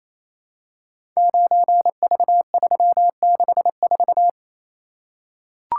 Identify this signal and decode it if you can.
Morse code, heard as a keyed tone, about 27 wpm, 700 Hz. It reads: 9V364
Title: Code Groups - 5 Letter:Numbers - 28wpm